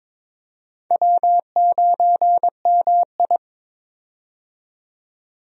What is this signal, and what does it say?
Morse code, heard as a keyed tone, about 22 wpm, 700 Hz. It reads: W9MI
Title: USA Call Signs - 4 Characters - 22wpm